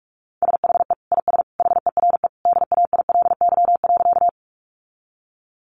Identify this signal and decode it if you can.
Morse code, heard as a keyed tone, about 44 wpm, 700 Hz. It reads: SHE IS HERE DAILY.
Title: Sentences from Top 1100 Words 45wpm